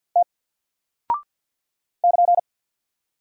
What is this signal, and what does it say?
Morse code, heard as a keyed tone, about 50 wpm, 700 Hz. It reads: T <KN>
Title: Single Character (Letter:Number:Punctuation:Prosign) 50wpm